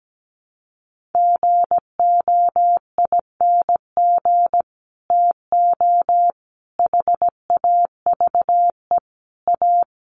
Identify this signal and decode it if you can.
Morse code, heard as a keyed tone, about 17 wpm, 700 Hz. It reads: GOING TO HAVE A
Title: Sets of 4 Words 17wpm